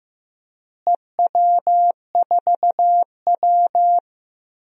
Morse code, heard as a keyed tone, about 15 wpm, 700 Hz. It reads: EW4W
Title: DXCC-100 Call Signs - 4 Characters - Repeated 3x - 15wpm